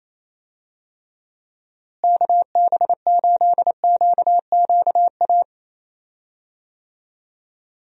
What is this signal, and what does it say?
Morse code, heard as a keyed tone, about 28 wpm, 700 Hz. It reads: KB8QQA
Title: USA Call Signs - 6 Characters - Repeated 2x - 28wpm